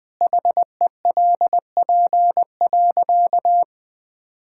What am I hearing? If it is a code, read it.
Morse code, heard as a keyed tone, about 20 wpm, 700 Hz. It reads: HELP.
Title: Sentences with 5-Word Phrases - 20wpm